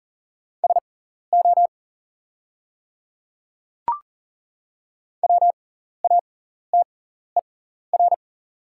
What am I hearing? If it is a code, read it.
Morse code, heard as a keyed tone, about 39 wpm, spaced as for 14 wpm, 700 Hz. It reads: SO WATER
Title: 6x Character Spacing - Top 100 Words - 40wpm